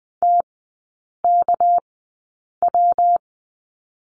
Morse code, heard as a keyed tone, about 20 wpm, spaced as for 8 wpm, 700 Hz. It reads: TKW
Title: 218 Lesson 35e - Number 0 - Three Characters 20wpm